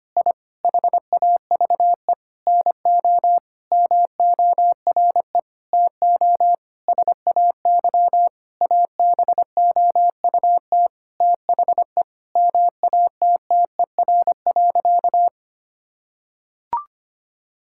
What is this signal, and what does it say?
Morse code, heard as a keyed tone, about 25 wpm, 700 Hz. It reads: I HAVE NO MORE TO SAY ABOUT THE MATTER.
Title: Sentences from Top 1000 Words - Review Part 5 - 25wpm